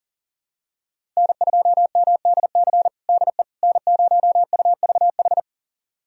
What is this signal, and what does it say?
Morse code, heard as a keyed tone, about 40 wpm, 700 Hz. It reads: N2MDC DE N0UVH
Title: USA Callsign DE Callsign - 5 Characters - 40wpm